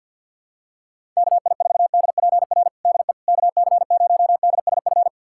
Morse code, heard as a keyed tone, about 49 wpm, 700 Hz. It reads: KI4DPR DE KC0DSF
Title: USA Callsign DE Callsign 50wpm